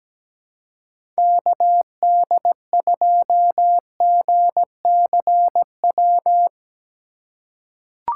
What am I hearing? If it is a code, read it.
Morse code, heard as a keyed tone, about 17 wpm, 700 Hz. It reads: KD2GCW
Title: USA Call Signs - 6 Characters - 4x Character Spacing - 17wpm